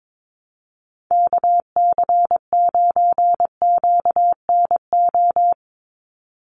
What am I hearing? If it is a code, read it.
Morse code, heard as a keyed tone, about 22 wpm, 700 Hz. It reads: KC9QNO
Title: USA Call Signs 22wpm